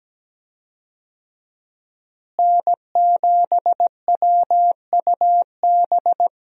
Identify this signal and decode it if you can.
Morse code, heard as a keyed tone, about 17 wpm, 700 Hz. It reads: N7WUB
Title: USA Call Signs - 5 Characters - Repeated 2x - 17wpm